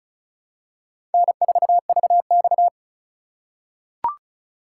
Morse code, heard as a keyed tone, about 34 wpm, 700 Hz. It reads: N4VX
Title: USA Call Signs - 4 Characters - 4x Character Spacing - 35wpm